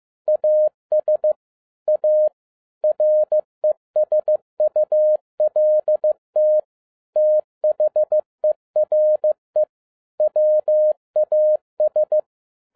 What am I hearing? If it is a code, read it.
Morse code, heard as a keyed tone, about 15 wpm, 610 Hz. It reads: AS A RESULT THERE WAS
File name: FIELD DAY CONTEST QRV FROM SAKAE VILLAGE 15wpm